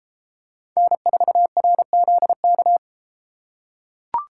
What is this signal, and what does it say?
Morse code, heard as a keyed tone, about 32 wpm, 700 Hz. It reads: N4RZK